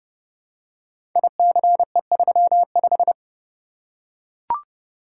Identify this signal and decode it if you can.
Morse code, heard as a keyed tone, about 30 wpm, 700 Hz. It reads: ICE35
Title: Code Groups - 5 Letter:Numbers - 30wpm